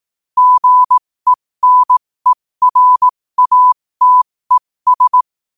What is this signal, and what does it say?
Morse code, heard as a keyed tone, about 18 wpm, spaced as for 15 wpm, 1000 Hz. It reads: GENERATES
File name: qotdfiles / 15wpm